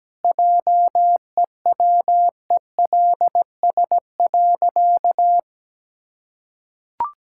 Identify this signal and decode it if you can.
Morse code, heard as a keyed tone, about 17 wpm, 700 Hz. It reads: JEWELS.
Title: Pangram Challenge v2 - 17wpm